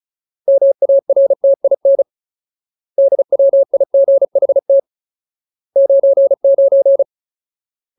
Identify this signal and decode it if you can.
Morse code, heard as a keyed tone, about 34 wpm, 550 Hz. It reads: MARTIN DWIGHT 99
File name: Ultimate Word List / 35WPM Ultimate